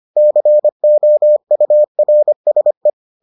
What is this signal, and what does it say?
Morse code, heard as a keyed tone, about 25 wpm, 600 Hz. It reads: COURSE